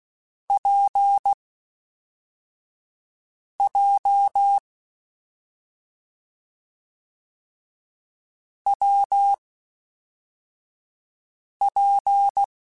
Morse code, heard as a keyed tone, about 16 wpm, 800 Hz. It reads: P J W P
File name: Difficult Combos / Difficult Combos-mp3s / WPJ WPJ1 @5